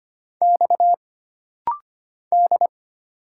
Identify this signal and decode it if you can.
Morse code, heard as a keyed tone, about 25 wpm, 700 Hz. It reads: X D
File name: Single Character (Letter:Number:Punctuation:Prosign) - Rapid-Fire - 25wpm